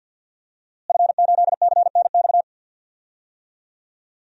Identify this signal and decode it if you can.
Morse code, heard as a keyed tone, about 51 wpm, 700 Hz. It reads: F8CNX
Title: DXCC-100 Call Signs - 5 Characters - Repeated 2x - 50wpm